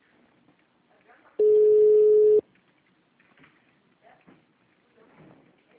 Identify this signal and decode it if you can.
Ringback tone (Europe, 425 Hz), with speech.